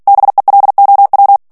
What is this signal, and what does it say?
Morse code, heard as a keyed tone, about 48 wpm, 800 Hz. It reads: BELOW
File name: ANGLAIS texte clair 3 vitesse 45 mots minute